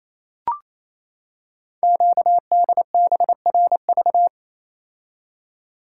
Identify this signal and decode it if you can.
Morse code, heard as a keyed tone, about 28 wpm, 700 Hz. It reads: QDBRV
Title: Code Groups - 5 Letters - 28wpm